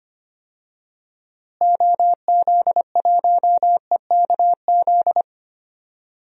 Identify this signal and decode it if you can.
Morse code, heard as a keyed tone, about 25 wpm, 700 Hz. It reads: OZ1EKZ
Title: DXCC-100 Call Signs - 6 Characters - Repeated 3x - 25wpm